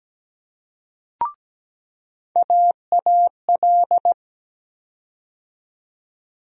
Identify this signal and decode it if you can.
Morse code, heard as a keyed tone, about 17 wpm, 700 Hz. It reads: AAL
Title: USA Call Sign Postfixes - 3 Letters 17wpm